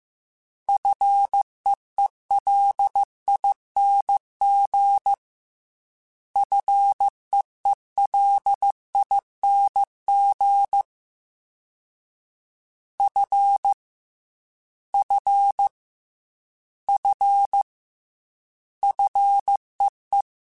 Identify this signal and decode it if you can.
Morse code, heard as a keyed tone, about 15 wpm, 800 Hz. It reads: FEELING FEELING F F F FEE
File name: F and L / F and L mp3 / F and L Prefixes 3x2 Twice @13